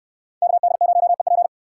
Morse code, heard as a keyed tone, about 63 wpm, 700 Hz. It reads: KD0EP